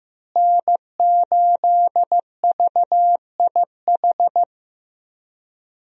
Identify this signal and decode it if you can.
Morse code, heard as a keyed tone, about 15 wpm, 700 Hz. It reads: N8VIH
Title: USA Call Signs - 5 Characters - Repeated 2x - 15wpm